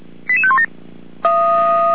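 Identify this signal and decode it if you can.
Five-tone selective call (ZVEI), heard as quick runs of five tones, tones of about 70 ms. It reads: 88517
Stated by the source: ZVEI five-tone selective call received on an SDR.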